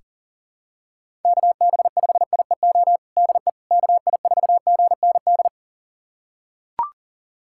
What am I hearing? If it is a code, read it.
Morse code, heard as a keyed tone, about 40 wpm, 700 Hz. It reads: KB5IEO DE KI4GND
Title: USA Callsign DE Callsign 40wpm